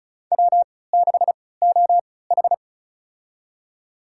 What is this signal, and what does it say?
Morse code, heard as a keyed tone, about 36 wpm, spaced as for 20 wpm, 700 Hz. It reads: W6OH
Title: USA Call Signs - 4 Characters - 3x Character Spacing - 35wpm